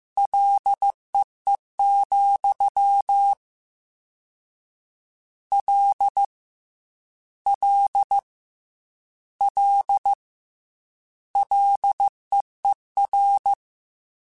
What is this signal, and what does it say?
Morse code, heard as a keyed tone, about 15 wpm, 800 Hz. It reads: LEE, L L L LEER
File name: Combinations / Combinations-mp3s / Combinations@13